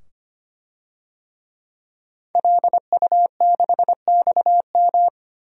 Morse code, heard as a keyed tone, about 25 wpm, 700 Hz. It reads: LU6XM
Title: DXCC-100 Call Signs - 5 Characters - 25wpm